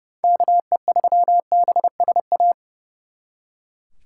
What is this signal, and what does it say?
Morse code, heard as a keyed tone, about 36 wpm, 700 Hz. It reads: KE3BSA
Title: Speed Racing - USA 2x3 Call Signs 30wpm